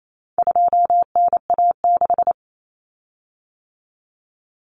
Morse code, heard as a keyed tone, about 28 wpm, 700 Hz. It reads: 2NA6
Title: Common Cut Numbers - 4-Digit 28wpm